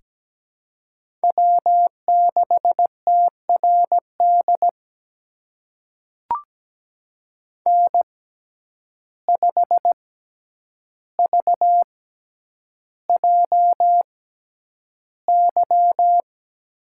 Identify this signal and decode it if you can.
Morse code, heard as a keyed tone, about 17 wpm, spaced as for 9 wpm, 700 Hz. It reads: W6TRD N5VJY
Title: USA Call Signs - 5 Characters - 6x Character Spacing - 17wpm